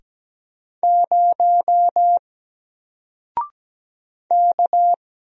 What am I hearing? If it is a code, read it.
Morse code, heard as a keyed tone, about 17 wpm, 700 Hz. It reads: 0 K
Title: Single Character (Letter:Number:Punctuation:Prosign) 17wpm